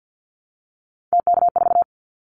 Morse code, heard as a keyed tone, about 50 wpm, 700 Hz. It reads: TK4
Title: Callsign Prefixes for Top 100 DXCC - 50wpm